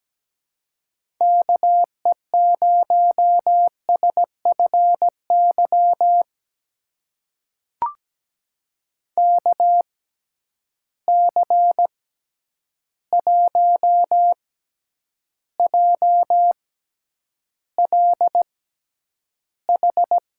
Morse code, heard as a keyed tone, about 17 wpm, spaced as for 9 wpm, 700 Hz. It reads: KE0SFY KC1JLH
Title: USA Call Signs - 6 Characters - 6x Character Spacing - 17wpm